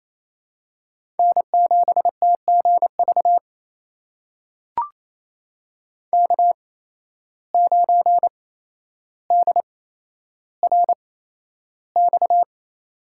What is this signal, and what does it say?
Morse code, heard as a keyed tone, about 28 wpm, spaced as for 12 wpm, 700 Hz. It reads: N7TGV K9DRX
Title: USA Call Signs - 5 Characters - 8x Character Spacing - 28wpm